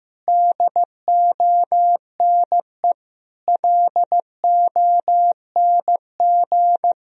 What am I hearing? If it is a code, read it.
Morse code, heard as a keyed tone, about 15 wpm, 700 Hz. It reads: DONE LONG
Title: Sentences from Top 1000 Words - Review Part 3 - 15wpm